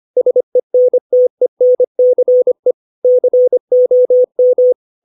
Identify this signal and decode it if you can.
Morse code, heard as a keyed tone, about 25 wpm, 500 Hz. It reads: SENTENCE COM
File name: 632 words / 632 Words 25WPM